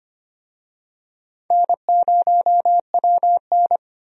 Morse code, heard as a keyed tone, about 25 wpm, 700 Hz. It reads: N0WN
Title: USA Call Signs - 4 Characters - 25wpm